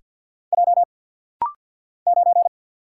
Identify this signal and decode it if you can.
Morse code, heard as a keyed tone, about 50 wpm, 700 Hz. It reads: J 9
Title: Single Letter-Number - Rapid-Fire 50wpm